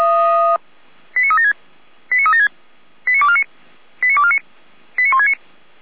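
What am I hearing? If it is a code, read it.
Five-tone selective call (ZVEI), heard as quick runs of five tones, tones of about 80 ms. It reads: 89376 89376 89249 89249 89169
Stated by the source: ZVEI five-tone selective call received with an SDR, recorded by Feuerwehr Orlamünde fire brigade.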